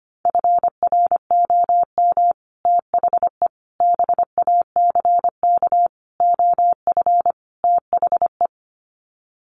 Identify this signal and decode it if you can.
Morse code, heard as a keyed tone, about 25 wpm, 700 Hz. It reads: FROM THE BACK OF THE
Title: Sets of 5 Words Encore 25wpm